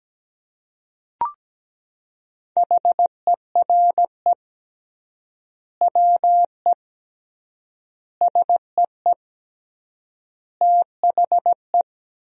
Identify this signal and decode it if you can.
Morse code, heard as a keyed tone, about 17 wpm, 700 Hz. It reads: HERE WE SEE THE
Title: 3x Word-Spacing - Sentences from Top 300 Words - 17wpm